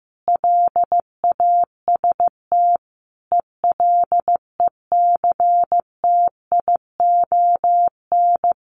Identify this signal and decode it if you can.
Morse code, heard as a keyed tone, about 15 wpm, 700 Hz. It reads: LAST ELECTION
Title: Sentences from Top 1500 Words 15wpm